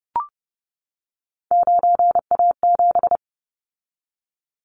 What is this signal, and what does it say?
Morse code, heard as a keyed tone, about 30 wpm, 700 Hz. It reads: 9A7
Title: Callsign Prefixes for Top 100 DXCC - 30wpm